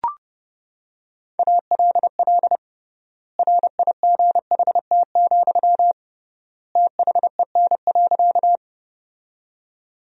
Morse code, heard as a keyed tone, about 32 wpm, 700 Hz. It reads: ALL RIGHT, THEN.
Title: 3x Word-Spacing - Sentences from Top 200 Words - 30wpm